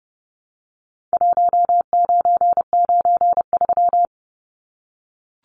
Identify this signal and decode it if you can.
Morse code, heard as a keyed tone, about 30 wpm, 700 Hz. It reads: 1993
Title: Numbers - Contest Style - 4-Digit 30wpm